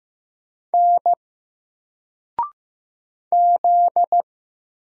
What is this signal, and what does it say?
Morse code, heard as a keyed tone, about 15 wpm, 700 Hz. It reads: N Z
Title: Single Letter-Number Alt 15wpm